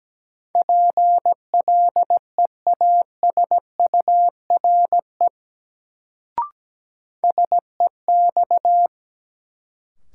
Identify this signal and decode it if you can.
Morse code, heard as a keyed tone, about 17 wpm, 700 Hz. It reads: PLEASURE SEX
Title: Sentences from Top 1600 Words - New Word Drill Alt - 17wpm